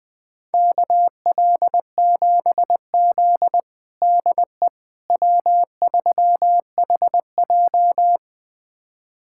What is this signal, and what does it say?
Morse code, heard as a keyed tone, about 21 wpm, 700 Hz. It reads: KL7Z DE W3HJ
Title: USA Callsign DE Callsign - 4 Characters - 20wpm